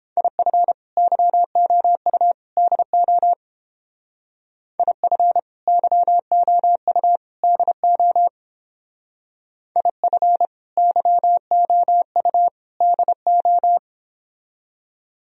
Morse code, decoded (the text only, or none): IF YOU DO IF YOU DO IF YOU DO